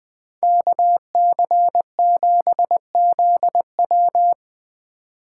Morse code, heard as a keyed tone, about 21 wpm, 700 Hz. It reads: KC7ZW